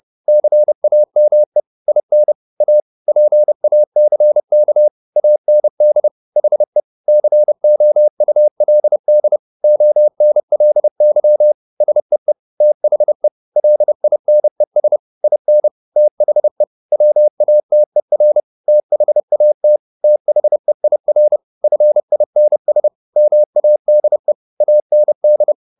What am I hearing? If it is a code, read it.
Morse code, heard as a keyed tone, about 30 wpm, 600 Hz. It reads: CAME IN A PACK AND HE COULD ONLY SEE THE LINES IN THE WATER THAT THEIR FINS MADE AND